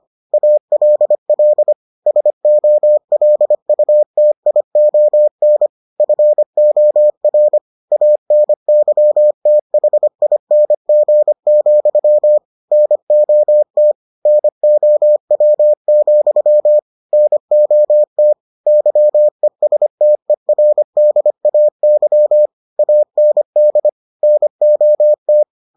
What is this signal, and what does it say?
Morse code, heard as a keyed tone, about 25 wpm, 600 Hz. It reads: ALL SOLUTION FOR ANYTHING, NOT NOW, NOT YESTERDAY AND NOT